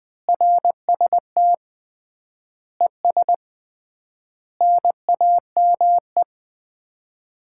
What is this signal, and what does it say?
Morse code, heard as a keyed tone, about 20 wpm, 700 Hz. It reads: RST ES NAME
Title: Common QSO Phrases - 3x Word-Spacing - 20wpm